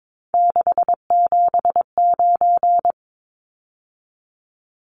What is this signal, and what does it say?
Morse code, heard as a keyed tone, about 22 wpm, 700 Hz. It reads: 679
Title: Numbers - 3-Digit 22wpm